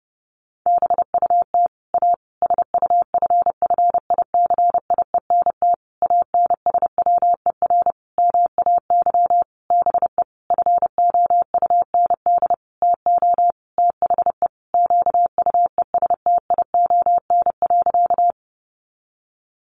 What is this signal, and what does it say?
Morse code, heard as a keyed tone, about 30 wpm, 700 Hz. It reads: BUT A SUFFICIENT ANSWER MAY BE FOUND TO THE QUESTION.
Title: Sentences from Top 1400 Words 30wpm